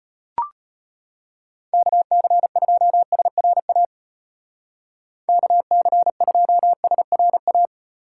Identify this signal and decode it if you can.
Morse code, heard as a keyed tone, about 32 wpm, 700 Hz. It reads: KC2SRA KC2SRA